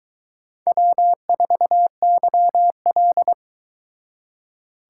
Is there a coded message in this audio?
Yes: Morse code from 0.7 to 3.3 s, heard as a keyed tone, about 22 wpm, 700 Hz. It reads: W4YL